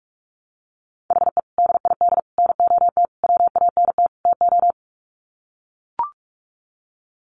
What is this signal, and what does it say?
Morse code, heard as a keyed tone, about 44 wpm, 700 Hz. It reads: HE DID NOT WANT TO
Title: Sets of 5 Words Encore 45wpm